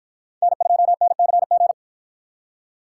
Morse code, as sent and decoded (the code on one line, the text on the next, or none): -. ..--- -. -.-. --.
N2NCG